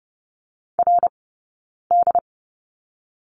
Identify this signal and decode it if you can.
Morse code, heard as a keyed tone, about 30 wpm, 700 Hz. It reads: R D
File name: course / 30:10wpm / 086 Lesson 15d - Number 5 - Two Characters 30wpm